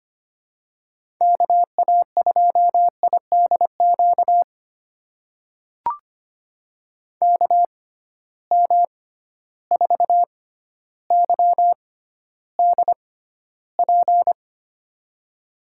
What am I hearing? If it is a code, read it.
Morse code, heard as a keyed tone, about 25 wpm, spaced as for 13 wpm, 700 Hz. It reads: KA2IDQ KM4YDP
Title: USA Call Signs - 6 Characters - 6x Character Spacing - 25wpm